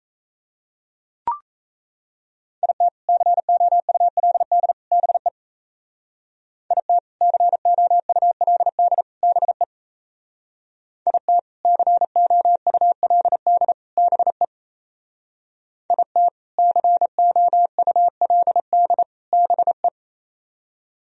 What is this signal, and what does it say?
Morse code, heard as a keyed tone, about 34 wpm, 700 Hz. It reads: IT COULD BE IT COULD BE IT COULD BE IT COULD BE
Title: Speed-Racing - Sets of 3 Words 28wpm